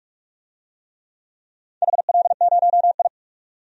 Morse code, heard as a keyed tone, about 44 wpm, 700 Hz. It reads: HL0I
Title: DXCC-100 Call Signs - 4 Characters - 45wpm